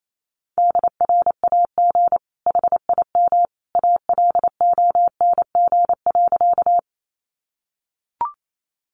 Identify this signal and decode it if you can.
Morse code, heard as a keyed tone, about 28 wpm, 700 Hz. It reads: DRAG HIM ALONG.
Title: Sentences from Top 1500 Words 28wpm